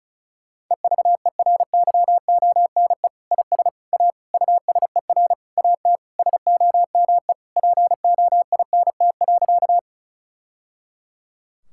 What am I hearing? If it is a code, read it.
Morse code, heard as a keyed tone, about 36 wpm, 700 Hz. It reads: EVERYONE IS A USER AT SOME POINT.
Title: Sentences from Top 1000 Words - Review Part 5 - 35wpm